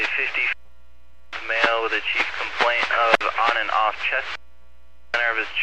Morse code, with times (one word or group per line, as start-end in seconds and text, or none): none